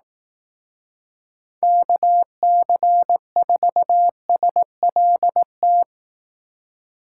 Morse code, heard as a keyed tone, about 18 wpm, 700 Hz. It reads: KC4SLT
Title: Speed Racing - USA 2x3 Call Signs 15wpm